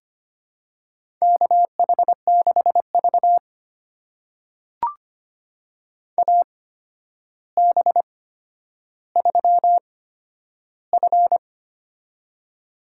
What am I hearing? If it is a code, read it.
Morse code, heard as a keyed tone, about 25 wpm, spaced as for 11 wpm, 700 Hz. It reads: KH6V AB3F
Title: USA Call Signs - 4 Characters - 8x Character Spacing - 25wpm